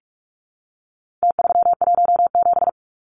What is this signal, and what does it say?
Morse code, heard as a keyed tone, about 44 wpm, 700 Hz. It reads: T317
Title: Numbers - Contest Style - 4-Digit 45wpm